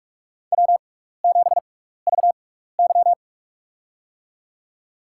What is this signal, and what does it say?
Morse code, heard as a keyed tone, about 46 wpm, spaced as for 15 wpm, 700 Hz. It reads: W7VY
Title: USA Call Signs - 4 Characters - 6x Character Spacing - 45wpm